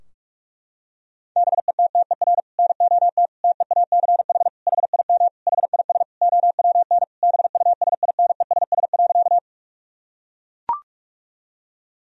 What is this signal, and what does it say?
Morse code, heard as a keyed tone, about 41 wpm, 700 Hz. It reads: BETTER NOT TEACH HIM HIS OWN BUSINESS.
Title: Sentences from Top 1000 Words - Review Part 2 - 45wpm